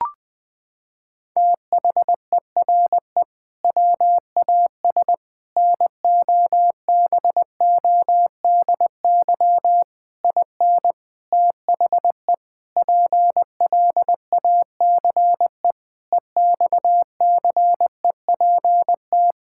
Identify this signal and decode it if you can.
Morse code, heard as a keyed tone, about 20 wpm, 700 Hz. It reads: THERE WAS NOBODY IN THE PLACE EXCEPT
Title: Sentences from Top 1600 Words 20wpm